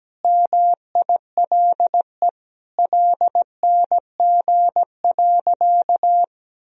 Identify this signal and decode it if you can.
Morse code, heard as a keyed tone, about 17 wpm, 700 Hz. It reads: MILE LNG.
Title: Sentences from Top 500 Words and Abbreviations 17wpm